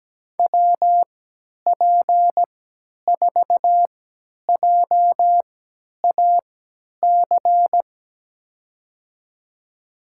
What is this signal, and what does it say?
Morse code, heard as a keyed tone, about 17 wpm, spaced as for 10 wpm, 700 Hz. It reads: WP4JAC
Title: USA Call Signs - 3x Character Spacing - 17wpm